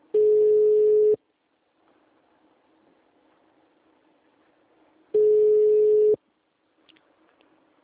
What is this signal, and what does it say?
Ringback tone (Europe, 425 Hz).